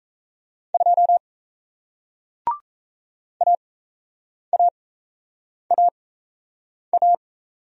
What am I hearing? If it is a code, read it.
Morse code, heard as a keyed tone, about 38 wpm, spaced as for 8 wpm, 700 Hz. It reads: 2 AAAA